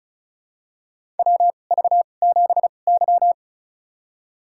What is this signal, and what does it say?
Morse code, heard as a keyed tone, about 34 wpm, spaced as for 25 wpm, 700 Hz. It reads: WV7Y
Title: USA Call Signs - 4 Characters - 2x Character Spacing - 35wpm